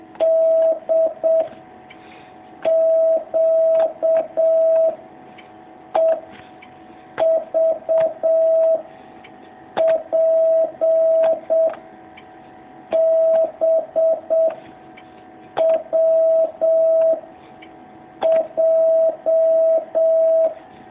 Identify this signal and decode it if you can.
Morse code, heard as a keyed tone, about 7 wpm, 650 Hz. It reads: D Q E V P B W J
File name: jh8cbh / 2019cw